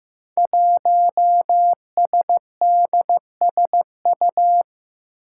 Morse code, heard as a keyed tone, about 15 wpm, 700 Hz. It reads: 1SDSU
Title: Five Challenging Letter-Number Combinations 15wpm